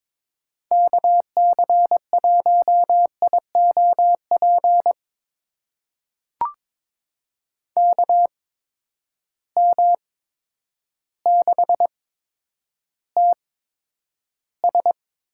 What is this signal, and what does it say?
Morse code, heard as a keyed tone, about 22 wpm, spaced as for 10 wpm, 700 Hz. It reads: KC1IOP KM6TS
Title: USA Call Signs - 6 Characters - 8x Character Spacing - 22wpm